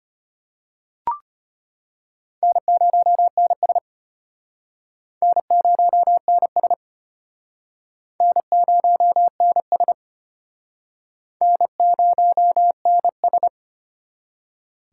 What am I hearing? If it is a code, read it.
Morse code, heard as a keyed tone, about 33 wpm, 700 Hz. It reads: N0NS N0NS N0NS N0NS